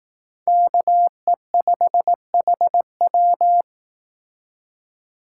Morse code, heard as a keyed tone, about 18 wpm, 700 Hz. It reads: KE5HW
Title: Speed Racing - USA 2x2 Call Signs 15wpm